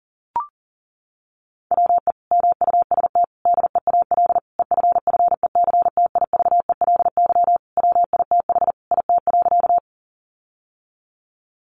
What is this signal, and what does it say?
Morse code, heard as a keyed tone, about 40 wpm, 700 Hz. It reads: WE MUST DEAL EFFECTIVELY WITH IT.
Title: Sentences from Top 1400 Words 40wpm